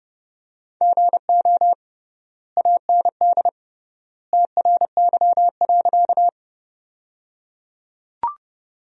Morse code, heard as a keyed tone, about 30 wpm, 700 Hz. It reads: GO AND TRY.
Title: 3x Word-Spacing - Sentences from Top 200 Words - 30wpm